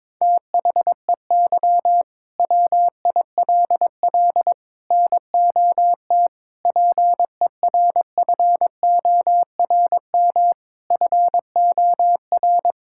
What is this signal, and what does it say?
Morse code, heard as a keyed tone, about 22 wpm, 700 Hz. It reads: THEY WILL NOT PERFORM FOR
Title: Sentences from Top 1000 Words - Review Part 5 - 22wpm